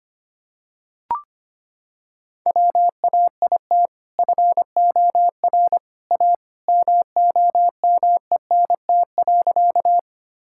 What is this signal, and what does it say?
Morse code, heard as a keyed tone, about 25 wpm, 700 Hz. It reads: WAIT FOR A MOMENT.
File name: practice / 25wpm / Sentences with 2-Word Phrases - 25wpm